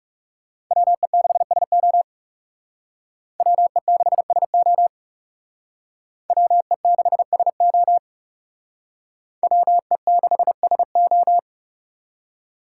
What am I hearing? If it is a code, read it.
Morse code, heard as a keyed tone, about 37 wpm, 700 Hz. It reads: WE6SO WE6SO WE6SO WE6SO
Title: Speed Racing - USA Call Signs 30wpm